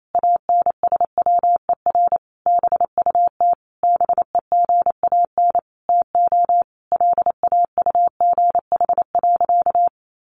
Morse code, heard as a keyed tone, about 28 wpm, 700 Hz. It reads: ANSWER BUT BEGAN TO LAUGH.
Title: Sentences from Top 600 Words 28wpm